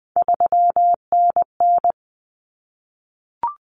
Morse code, heard as a keyed tone, about 20 wpm, 700 Hz. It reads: 3NN
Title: Common Cut Numbers - 3-Digit 20wpm